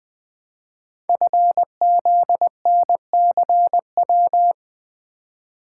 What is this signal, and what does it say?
Morse code, heard as a keyed tone, about 20 wpm, 700 Hz. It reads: FZNCW
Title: Code Groups - 5 Letters - 20wpm